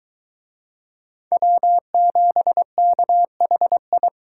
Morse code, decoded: W7KHI

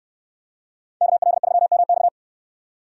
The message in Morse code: KL3MX

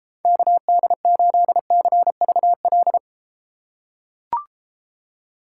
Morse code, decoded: KD8CVL

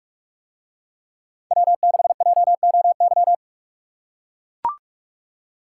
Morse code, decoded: W6JOY

